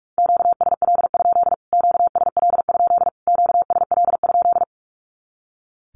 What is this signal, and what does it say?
Morse code, heard as a keyed tone, about 44 wpm, 700 Hz. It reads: QSL? QSL? QSL?
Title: Q-Codes Repeated 3x 45wpm